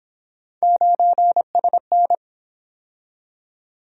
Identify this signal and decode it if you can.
Morse code, heard as a keyed tone, about 28 wpm, 700 Hz. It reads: 9SN